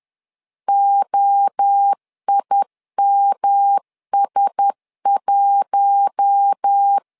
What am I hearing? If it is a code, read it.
Morse code, heard as a keyed tone, about 10 wpm, 800 Hz. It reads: OIMS1
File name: LEZIONI / 4 Q1S 10wpm 2133